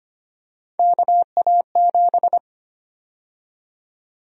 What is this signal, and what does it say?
Morse code, heard as a keyed tone, about 25 wpm, 700 Hz. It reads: KA7
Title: USA Call Sign Prefixes - 25wpm